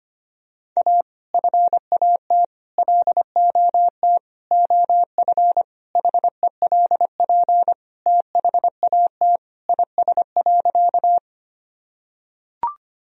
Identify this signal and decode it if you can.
Morse code, heard as a keyed tone, about 25 wpm, 700 Hz. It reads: A FAT LOT OF HELP THAT IS.
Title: Sentences from Top 1000 Words - Review Part 4 - 25wpm